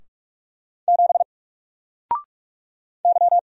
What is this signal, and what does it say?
Morse code, heard as a keyed tone, about 44 wpm, 700 Hz. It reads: 7 Y
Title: Single Letter-Number 45wpm